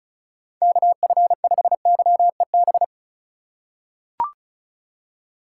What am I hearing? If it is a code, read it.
Morse code, heard as a keyed tone, about 36 wpm, 700 Hz. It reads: KF5YEB
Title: USA Call Signs - 6 Characters - 3x Character Spacing - 35wpm